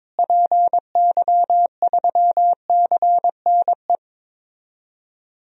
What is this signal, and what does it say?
Morse code, heard as a keyed tone, about 22 wpm, 700 Hz. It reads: PY3CNE
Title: DXCC-100 Call Signs - 6 Characters - Repeated 3x - 22wpm